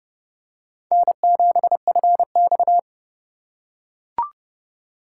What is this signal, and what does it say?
Morse code, heard as a keyed tone, about 29 wpm, 700 Hz. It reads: N7FX